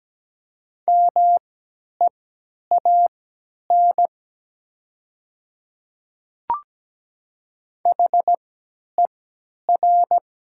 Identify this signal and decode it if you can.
Morse code, heard as a keyed tone, about 17 wpm, spaced as for 10 wpm, 700 Hz. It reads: MEAN HER
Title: 3x Character Spacing - Top 200 Words - 17wpm